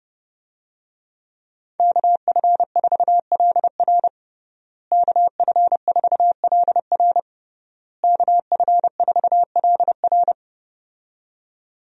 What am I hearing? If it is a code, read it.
Morse code, heard as a keyed tone, about 30 wpm, 700 Hz. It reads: KF4LR KF4LR KF4LR
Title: USA Call Signs - 5 Characters - Repeated 3x - 30wpm